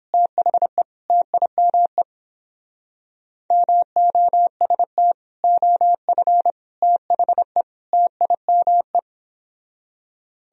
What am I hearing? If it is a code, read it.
Morse code, heard as a keyed tone, about 29 wpm, 700 Hz. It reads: THE TIME MOST OF THE TIME